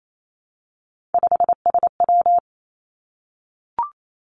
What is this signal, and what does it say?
Morse code, heard as a keyed tone, about 28 wpm, 700 Hz. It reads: 5SW
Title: Three Challenging Letter-Number Combinations 28wpm